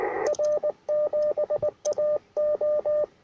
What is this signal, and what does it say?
Morse code, heard as a keyed tone, about 20 wpm, 600 Hz, with radio noise. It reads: R7AO
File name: R7AO UR4MCK